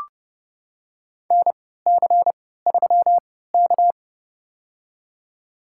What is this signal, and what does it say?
Morse code, heard as a keyed tone, about 31 wpm, spaced as for 17 wpm, 700 Hz. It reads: NC3K
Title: USA Call Signs - 4 Characters - 3x Character Spacing - 30wpm